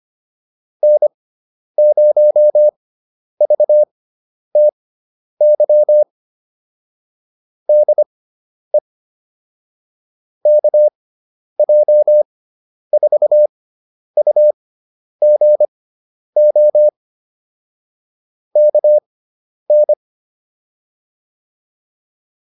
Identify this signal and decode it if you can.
Morse code, heard as a keyed tone, about 25 wpm, spaced as for 10 wpm, 600 Hz. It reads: N0VTY DE KJ4UGO KN